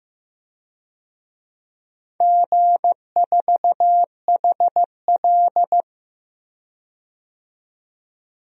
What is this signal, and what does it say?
Morse code, heard as a keyed tone, about 16 wpm, 700 Hz. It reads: G4HL
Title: DXCC-100 Call Signs - 4 Characters - 15wpm